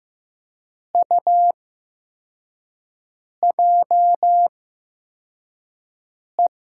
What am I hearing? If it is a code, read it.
Morse code, heard as a keyed tone, about 15 wpm, spaced as for 4 wpm, 700 Hz. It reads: UJE
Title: USA Call Signs - 6 Characters - 8x Character Spacing - 15wpm